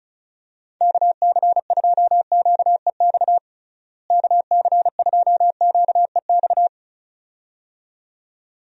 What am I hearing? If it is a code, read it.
Morse code, heard as a keyed tone, about 34 wpm, 700 Hz. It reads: KC2QEX KC2QEX
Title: USA Call Signs - 6 Characters - Repeated 2x - 35wpm